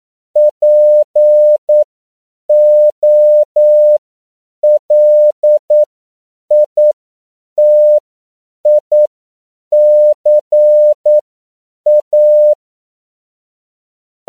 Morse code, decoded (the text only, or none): POLITICA